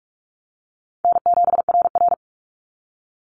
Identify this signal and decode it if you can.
Morse code, heard as a keyed tone, about 46 wpm, 700 Hz. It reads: N7RR
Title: NA SOTA Activator Callsigns 45wpm